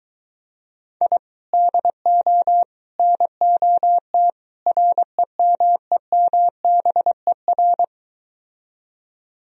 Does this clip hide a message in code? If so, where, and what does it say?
Yes: Morse code from 1.0 to 7.8 s, heard as a keyed tone, about 22 wpm, 700 Hz. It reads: I DO NOT REMEMBER